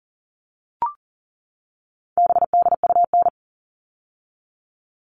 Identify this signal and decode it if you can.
Morse code, heard as a keyed tone, about 39 wpm, 700 Hz. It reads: BDUN
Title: All Cut Numbers - 4-Digit 40wpm